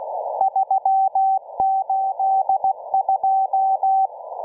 Morse code, heard as a keyed tone, about 16 wpm, 770 Hz. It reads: 382